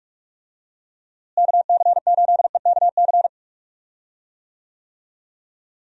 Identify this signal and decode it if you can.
Morse code, heard as a keyed tone, about 44 wpm, 700 Hz. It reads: KC8EKC
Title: USA Call Signs - 6 Characters - Repeated 2x - 45wpm